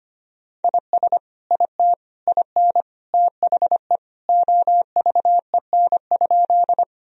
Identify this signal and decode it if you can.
Morse code, heard as a keyed tone, about 25 wpm, 700 Hz. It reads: IS IT IN THE OVEN?
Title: Sentences from Top 1000 Words - Review Part 4 - 25wpm